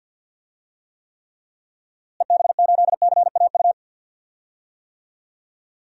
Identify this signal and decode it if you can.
Morse code, heard as a keyed tone, about 52 wpm, 700 Hz. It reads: EB8CAU
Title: DXCC-100 Call Signs - 6 Characters - Repeated 3x - 50wpm